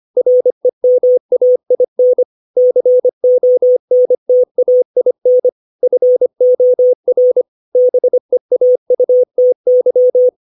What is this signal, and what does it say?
Morse code, heard as a keyed tone, about 25 wpm, 500 Hz. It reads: REMAIN CONTAIN FOR BEAUTY
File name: 632 words / 632 Words 25WPM